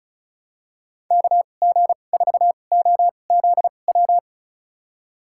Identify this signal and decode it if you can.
Morse code, heard as a keyed tone, about 36 wpm, spaced as for 25 wpm, 700 Hz. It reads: KG4OZW
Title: USA Call Signs - 6 Characters - 2x Character Spacing - 35wpm